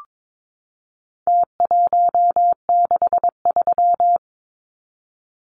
Morse code, heard as a keyed tone, about 22 wpm, 700 Hz. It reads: T163
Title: Numbers - Contest Style - 4-Digit 22wpm